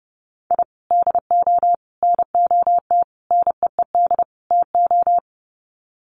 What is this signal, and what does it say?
Morse code, heard as a keyed tone, about 30 wpm, 700 Hz. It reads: I DO NOT NEED TO
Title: Sets of 5 Words 30wpm